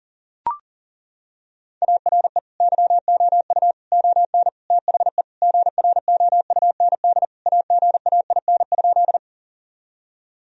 Morse code, heard as a keyed tone, about 40 wpm, 700 Hz. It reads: ARE YOU ON THE GROUND AGAIN?
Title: Sentences from Top 1000 Words - Review Part 2 - 40wpm